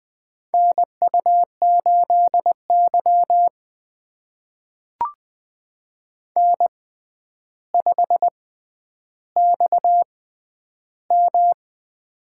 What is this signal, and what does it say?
Morse code, heard as a keyed tone, about 20 wpm, spaced as for 10 wpm, 700 Hz. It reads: NU8Y N5XM
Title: USA Call Signs - 4 Characters - 6x Character Spacing - 20wpm